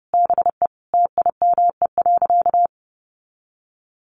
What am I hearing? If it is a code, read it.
Morse code, heard as a keyed tone, about 30 wpm, 700 Hz. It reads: BE TIME.
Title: Sentences from Top 200 Words 30wpm